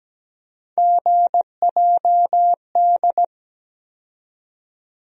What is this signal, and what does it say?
Morse code, heard as a keyed tone, about 17 wpm, 700 Hz. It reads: GJD
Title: USA Call Sign Postfixes - 3 Letters 17wpm